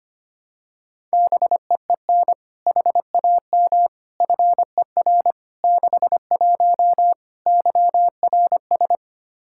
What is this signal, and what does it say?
Morse code, heard as a keyed tone, about 25 wpm, 700 Hz. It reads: BEEN HAM FER 61 YRS
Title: Practice QSO - Exchange 3 - More Info - 25wpm